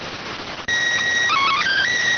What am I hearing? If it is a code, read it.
SSTV (slow-scan television), PD 290.